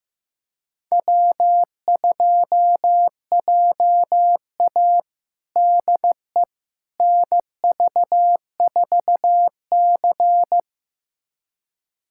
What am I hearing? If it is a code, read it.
Morse code, heard as a keyed tone, about 15 wpm, 700 Hz. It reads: W2JA DE NV4C
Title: USA Callsign DE Callsign - 4 Characters - 15wpm